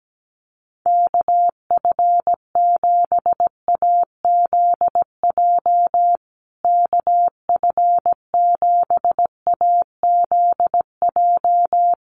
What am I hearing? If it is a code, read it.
Morse code, heard as a keyed tone, about 17 wpm, 700 Hz. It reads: KF7AZJ KF7AZJ
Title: USA Call Signs Repeated 3x 17wpm